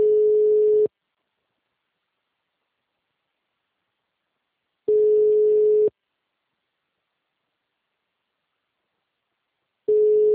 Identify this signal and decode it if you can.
Ringback tone (Europe, 425 Hz).
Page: Sumare - SP